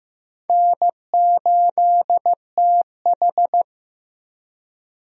Morse code, heard as a keyed tone, about 15 wpm, 700 Hz. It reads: N8TH
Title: USA Call Signs - 4 Characters - Repeated 3x - 15wpm